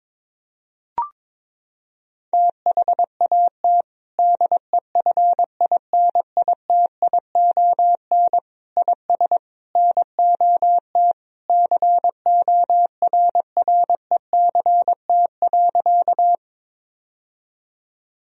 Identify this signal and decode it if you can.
Morse code, heard as a keyed tone, about 22 wpm, 700 Hz. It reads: THAT DEFINITION IS NOT CORRECT.
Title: Sentences from Top 1000 Words - Review Part 4 - 22wpm